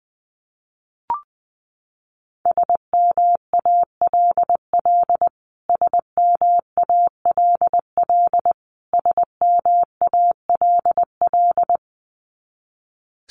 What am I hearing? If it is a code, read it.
Morse code, heard as a keyed tone, about 20 wpm, 700 Hz. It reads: SMALL SMALL SMALL
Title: Top 500 Words Repeated 20wpm